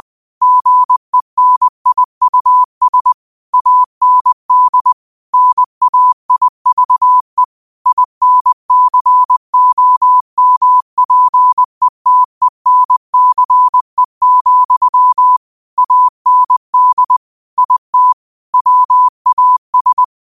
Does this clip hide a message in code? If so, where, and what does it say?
Yes: Morse code from 0.4 to 20.0 s, heard as a keyed tone, about 20 wpm, 1000 Hz. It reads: GENIUS AND NAIVE INCOMPETENCE, AND IT WAS